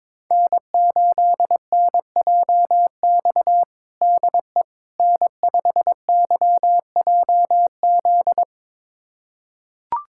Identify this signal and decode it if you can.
Morse code, heard as a keyed tone, about 22 wpm, 700 Hz. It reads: N8NJX DE N5YJZ
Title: USA Callsign DE Callsign - 5 Characters - 22wpm